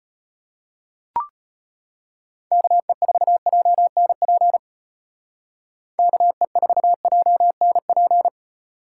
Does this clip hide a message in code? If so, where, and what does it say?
Yes: Morse code from 2.5 to 8.3 s, heard as a keyed tone, about 34 wpm, 700 Hz. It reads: KE4JNP KE4JNP